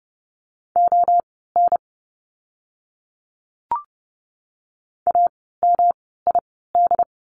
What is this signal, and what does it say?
Morse code, heard as a keyed tone, about 30 wpm, spaced as for 17 wpm, 700 Hz. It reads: ON AMID
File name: course / 30:10wpm / 108 Lesson 18f - Letter m - Words 30wpm